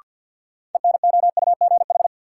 Morse code, heard as a keyed tone, about 53 wpm, 700 Hz. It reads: ENOUGH
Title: Sentences from Top 400 Words - New Word Drill - 50wpm